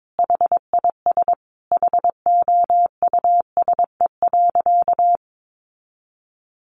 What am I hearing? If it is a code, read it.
Morse code, heard as a keyed tone, about 22 wpm, 700 Hz. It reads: HIS HOUSE.
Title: Easy Phrases 22wpm